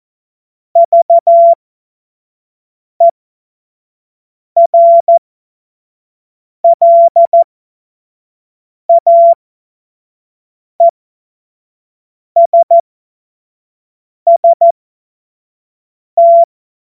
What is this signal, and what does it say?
Morse code, heard as a keyed tone, about 14 wpm, spaced as for 5 wpm, 680 Hz. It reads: VERLAESST
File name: Morsekurs / Witze / Witz 13. Juli 2021, 25 F70